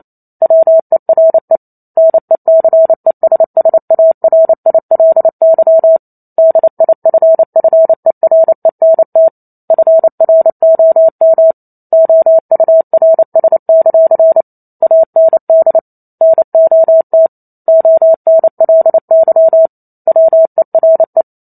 Morse code, heard as a keyed tone, about 28 wpm, 650 Hz. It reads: WERE NECESSARILY DIFFERENT FROM OURS; AND NOT ONLY WERE